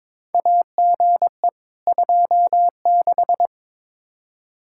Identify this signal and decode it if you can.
Morse code, heard as a keyed tone, about 22 wpm, 700 Hz. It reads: AGE 26
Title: Practice QSO - Exchange 3 - More Info - 22wpm